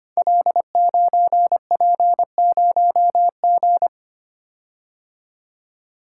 Morse code, heard as a keyed tone, about 25 wpm, 700 Hz. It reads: L9P0G
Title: Code Groups - 5 Letter:Numbers - 25wpm